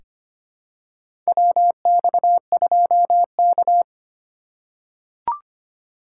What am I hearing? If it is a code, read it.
Morse code, heard as a keyed tone, about 25 wpm, 700 Hz. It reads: WX2K
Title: USA Call Signs - 4 Characters - 4x Character Spacing - 25wpm